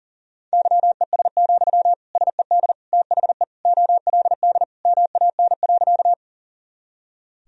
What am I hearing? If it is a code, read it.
Morse code, heard as a keyed tone, about 37 wpm, 700 Hz. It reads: YES, SED THE OLD MAN.
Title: Sentences from Top 500 Words and Abbreviations 40wpm